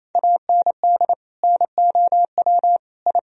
Morse code, heard as a keyed tone, about 28 wpm, 700 Hz. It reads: AND NOW I
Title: Sentences from Top 1000 Words - Review Part 1 - 28wpm